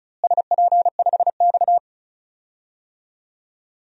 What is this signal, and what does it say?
Morse code, heard as a keyed tone, about 34 wpm, 700 Hz. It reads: SP5X
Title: DXCC-100 Call Signs - 4 Characters - Repeated 2x - 35wpm